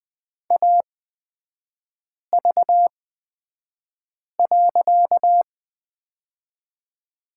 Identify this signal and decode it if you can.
Morse code, heard as a keyed tone, about 20 wpm, spaced as for 5 wpm, 690 Hz. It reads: AV.
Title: 255 Lesson 41e - Period - Three Characters 20wpm